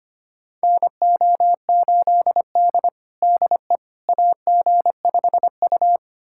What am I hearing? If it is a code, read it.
Morse code, heard as a keyed tone, about 25 wpm, 700 Hz. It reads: NO8D DE AG5U
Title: USA Callsign DE Callsign - 4 Characters - 25wpm